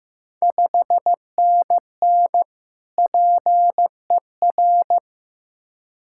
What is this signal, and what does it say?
Morse code, heard as a keyed tone, about 15 wpm, 700 Hz. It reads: 5NN PER
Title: Ontario QSO Party Exchange 15wpm